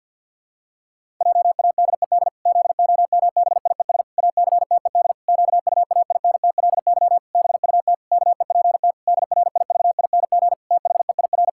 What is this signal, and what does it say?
Morse code, heard as a keyed tone, about 50 wpm, 700 Hz. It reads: JADED ZOMBIES ACTED QUAINTLY BUT KEPT DRIVING THEIR
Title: Pangram Challenge v2 - 50wpm